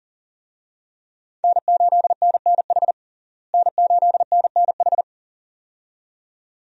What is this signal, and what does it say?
Morse code, heard as a keyed tone, about 40 wpm, 700 Hz. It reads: N8NNH N8NNH
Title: USA Call Signs - 5 Characters - Repeated 2x - 40wpm